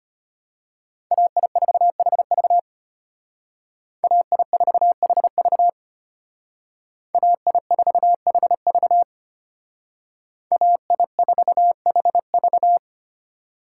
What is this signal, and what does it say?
Morse code, heard as a keyed tone, about 31 wpm, 700 Hz. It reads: AI4HV AI4HV AI4HV AI4HV
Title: Speed Racing - USA 2x2 Call Signs 25wpm